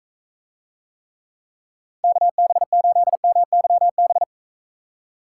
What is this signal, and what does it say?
Morse code, heard as a keyed tone, about 45 wpm, 700 Hz. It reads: KB8MYB